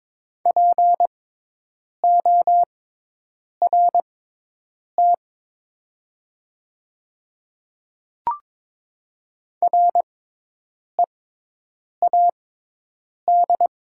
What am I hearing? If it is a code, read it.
Morse code, heard as a keyed tone, about 22 wpm, spaced as for 8 wpm, 700 Hz. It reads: PORT READ
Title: 6x Character Spacing - Top 500 Words - 22wpm